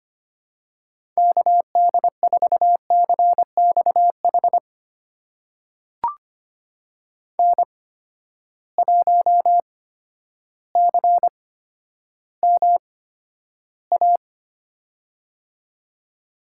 Morse code, heard as a keyed tone, about 25 wpm, spaced as for 11 wpm, 700 Hz. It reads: KD4CXH N1CMA
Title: USA Call Signs - 8x Character Spacing - 25wpm